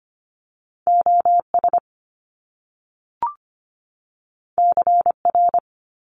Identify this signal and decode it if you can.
Morse code, heard as a keyed tone, about 25 wpm, 700 Hz. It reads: OS CR
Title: Top Two Letter Combinations in Words 25wpm